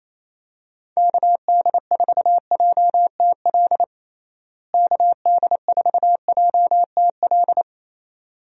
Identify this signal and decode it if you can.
Morse code, heard as a keyed tone, about 27 wpm, 700 Hz. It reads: KD4JTL KD4JTL
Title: USA Call Signs - 6 Characters - Repeated 2x - 28wpm